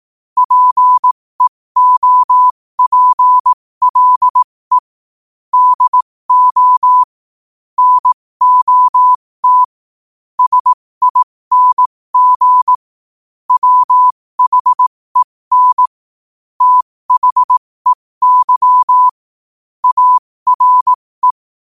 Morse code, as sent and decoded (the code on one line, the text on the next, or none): .--. --- .--. .-.. / -.. --- / -. --- - / ... .. -. --. / .-- .... -. / - .... -.-- / .- .-. .
PEOPLE DO NOT SING WHEN THEY ARE